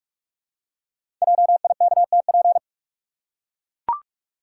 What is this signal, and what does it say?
Morse code, heard as a keyed tone, about 44 wpm, 700 Hz. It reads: JIKTP
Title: Code Groups - 5 Letters - 45wpm